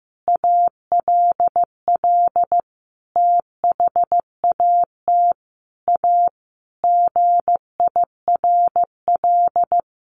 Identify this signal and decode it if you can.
Morse code, heard as a keyed tone, about 15 wpm, 700 Hz. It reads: ALL THAT A GIRL
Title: Sentences from Top 1400 Words 15wpm